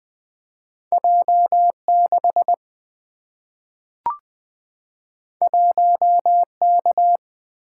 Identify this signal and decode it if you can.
Morse code, heard as a keyed tone, about 20 wpm, 700 Hz. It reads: J6 1K
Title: Single-Letter Single-Number 20wpm